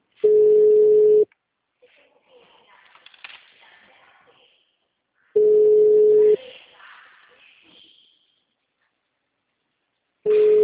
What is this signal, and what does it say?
Ringback tone (Europe, 425 Hz).
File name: out- -2008- - - .2